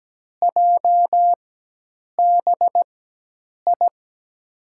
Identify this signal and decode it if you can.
Morse code, heard as a keyed tone, about 17 wpm, spaced as for 8 wpm, 700 Hz. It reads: JBI
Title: USA Call Signs - 4x Character Spacing - 17wpm